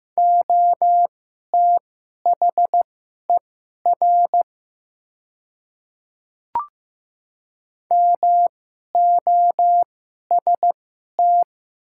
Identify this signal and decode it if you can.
Morse code, heard as a keyed tone, about 15 wpm, spaced as for 11 wpm, 700 Hz. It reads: OTHER MOST
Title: 2x Character Spacing - Top 200 Words - 15wpm